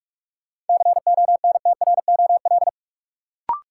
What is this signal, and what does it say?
Morse code, heard as a keyed tone, about 43 wpm, 700 Hz. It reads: CONTROL
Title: Sentences from Top 1200 Words - New Word Drill Alt - 45wpm